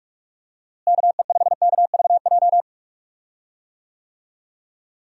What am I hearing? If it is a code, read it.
Morse code, heard as a keyed tone, about 41 wpm, 700 Hz. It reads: KE5KVJ
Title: USA Call Signs - 6 Characters - Repeated 3x - 45wpm